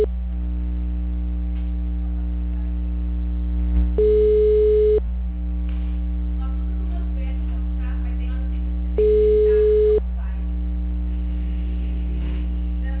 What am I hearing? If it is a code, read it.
Ringback tone (Europe, 425 Hz), with speech.